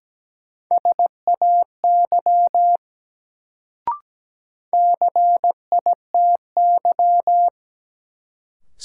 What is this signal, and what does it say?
Morse code, heard as a keyed tone, about 17 wpm, 700 Hz. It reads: SAY CITY
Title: Sets of 2 Words - New Word Drill - 17wpm